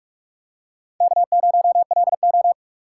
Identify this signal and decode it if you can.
Morse code, heard as a keyed tone, about 44 wpm, 690 Hz. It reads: K0LO
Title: USA Call Signs - 4 Characters - 2x Character Spacing - 45wpm